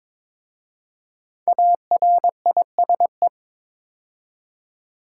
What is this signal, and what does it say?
Morse code, heard as a keyed tone, about 22 wpm, 700 Hz. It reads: ARISE
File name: 5 Letter Words - First 3 Letters Given - 22wpm